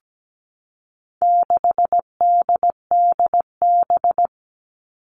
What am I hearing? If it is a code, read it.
Morse code, heard as a keyed tone, about 17 wpm, 700 Hz. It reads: 6DDB
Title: All Cut Numbers - 4-Digit 17wpm